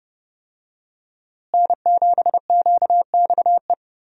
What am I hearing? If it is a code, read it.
Morse code, heard as a keyed tone, about 30 wpm, 700 Hz. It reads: N7QXE
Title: USA Call Signs - 5 Characters - Repeated 2x - 30wpm